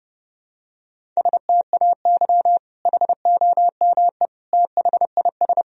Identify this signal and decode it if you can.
Morse code, heard as a keyed tone, about 30 wpm, 700 Hz. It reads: STAY HOME THIS
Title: Sentences from Top 1000 Words - Review Part 1 - 30wpm